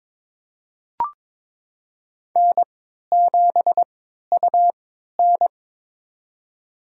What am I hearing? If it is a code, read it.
Morse code, heard as a keyed tone, about 22 wpm, spaced as for 13 wpm, 700 Hz. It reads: N7UN
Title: USA Call Signs - 3x Character Spacing - 22wpm